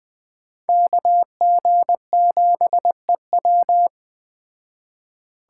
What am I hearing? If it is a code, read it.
Morse code, heard as a keyed tone, about 22 wpm, 700 Hz. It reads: KG7EW